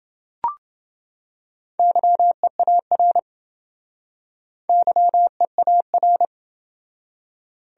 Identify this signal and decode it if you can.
Morse code, heard as a keyed tone, about 28 wpm, 700 Hz. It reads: YEAR YEAR